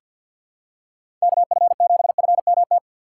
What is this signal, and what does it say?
Morse code, heard as a keyed tone, about 51 wpm, 700 Hz. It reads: KF7FMT
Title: USA Call Signs - 6x Character Spacing - 50wpm